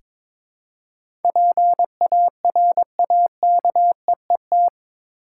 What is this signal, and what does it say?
Morse code, heard as a keyed tone, about 22 wpm, 700 Hz. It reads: PARAKEET
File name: Difficult Words with Hints - 22wpm